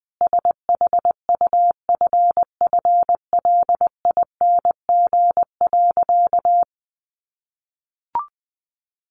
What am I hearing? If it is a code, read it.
Morse code, heard as a keyed tone, about 20 wpm, 700 Hz. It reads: SHUFFLING.
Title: A Sound Of Thunder 20wpm